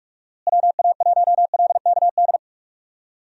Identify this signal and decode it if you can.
Morse code, heard as a keyed tone, about 41 wpm, 700 Hz. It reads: WA1LKD
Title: USA Call Signs - 6 Characters - Repeated 3x - 45wpm